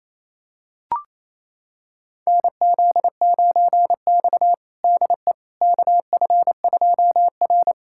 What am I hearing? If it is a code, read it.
Morse code, heard as a keyed tone, about 27 wpm, 700 Hz. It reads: NZ9X DE KF2R
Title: USA Callsign DE Callsign - 4 Characters - 28wpm